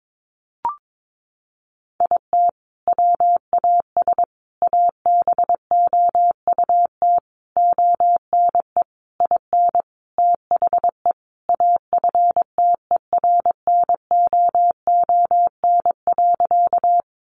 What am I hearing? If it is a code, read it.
Morse code, heard as a keyed tone, about 22 wpm, 700 Hz. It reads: IT WAS ABOUT ONE IN THE AFTERNOON.
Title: Sentences from Top 1500 Words 22wpm